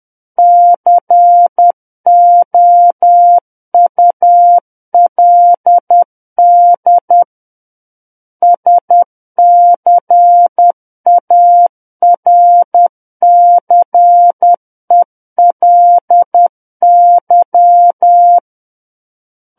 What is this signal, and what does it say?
Morse code, heard as a keyed tone, about 10 wpm, 700 Hz. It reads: COULD SCARCELY
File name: Book One / 01-Chapter 1 - 10 WPM